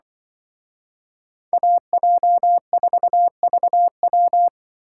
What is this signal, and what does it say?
Morse code, heard as a keyed tone, about 24 wpm, 700 Hz. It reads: AJ4VW
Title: Speed Racing - USA 2x2 Call Signs 20wpm